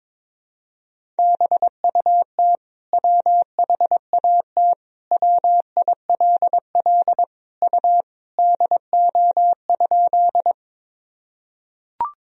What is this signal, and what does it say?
Morse code, heard as a keyed tone, about 22 wpm, 700 Hz. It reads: BUT WHAT WILL U DO?
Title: Sentences from Top 100 Words and Abbreviations 22wpm